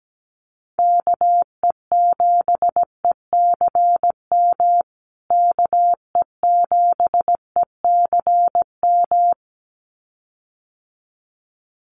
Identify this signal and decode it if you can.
Morse code, heard as a keyed tone, about 17 wpm, 700 Hz. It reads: KE7ECM KE7ECM
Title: USA Call Signs Repeated 2x 17wpm